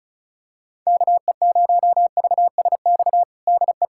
Morse code, heard as a keyed tone, about 36 wpm, 700 Hz. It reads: KE0VSX DE
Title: USA Callsign DE Callsign 35wpm